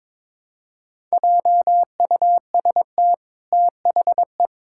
Morse code, heard as a keyed tone, about 22 wpm, 700 Hz. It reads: JUST THE
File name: practice / 22wpm / Sentences with 8-Word Phrases - 22wpm